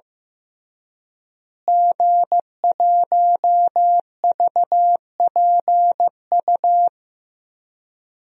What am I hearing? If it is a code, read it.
Morse code, heard as a keyed tone, about 15 wpm, 700 Hz. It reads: G1VPU
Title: DXCC-100 Call Signs - 5 Characters - Repeated 3x - 15wpm